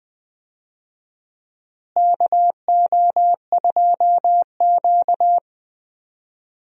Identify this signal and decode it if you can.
Morse code, heard as a keyed tone, about 20 wpm, 700 Hz. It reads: KO2Q
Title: USA Call Signs - 4 Characters - Repeated 2x - 20wpm